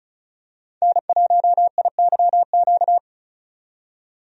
Morse code, heard as a keyed tone, about 36 wpm, 700 Hz. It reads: N1IYQ
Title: USA Call Signs - 5 Characters - 35wpm